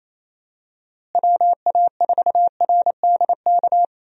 Morse code, heard as a keyed tone, about 28 wpm, 700 Hz. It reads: WA4RDK
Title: USA Call Signs - 6 Characters - Repeated 3x - 28wpm